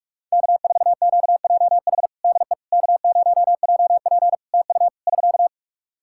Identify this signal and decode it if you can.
Morse code, heard as a keyed tone, about 44 wpm, 700 Hz. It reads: K4QJH DE K0JP TU <SK>
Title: Practice QSO - Exchange 4 - Ending - 45wpm